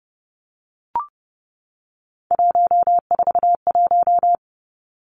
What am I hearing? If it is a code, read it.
Morse code, heard as a keyed tone, about 32 wpm, 700 Hz. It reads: 141
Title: Three Challenging Letter-Number Combinations 30wpm